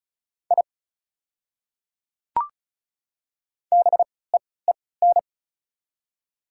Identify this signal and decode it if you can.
Morse code, heard as a keyed tone, about 36 wpm, spaced as for 20 wpm, 690 Hz. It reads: I BEEN
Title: 3x Character Spacing - Top 100 Words - 35wpm